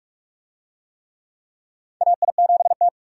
Morse code, heard as a keyed tone, about 46 wpm, 700 Hz. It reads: AI7T
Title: USA Call Signs - 4 Characters - 45wpm